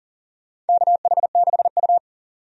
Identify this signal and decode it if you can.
Morse code, heard as a keyed tone, about 40 wpm, 700 Hz. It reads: KH6U
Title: DXCC-100 Call Signs - 4 Characters - 40wpm